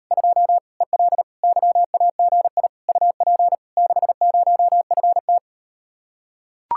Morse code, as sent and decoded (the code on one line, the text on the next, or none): ..--- / . .-.. / -.-- .- --. .. / ..- .--. / -.... ----- ..-. -
2 EL YAGI UP 60FT